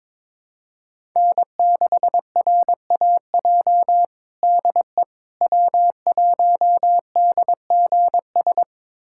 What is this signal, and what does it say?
Morse code, heard as a keyed tone, about 22 wpm, 700 Hz. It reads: N6RAJ DE W1DGS
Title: USA Callsign DE Callsign - 5 Characters - 22wpm